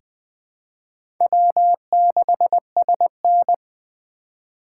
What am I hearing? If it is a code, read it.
Morse code, heard as a keyed tone, about 20 wpm, 700 Hz. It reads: W6SN
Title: USA Call Signs - 3x Character Spacing - 20wpm